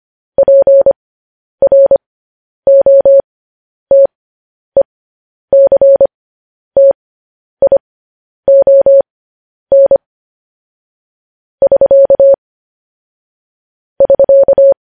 Morse code, heard as a keyed tone, about 25 wpm, spaced as for 10 wpm, 550 Hz. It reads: PROTECTION <SK> <SK>
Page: news-headlines-in-morse-code-at-10-wpm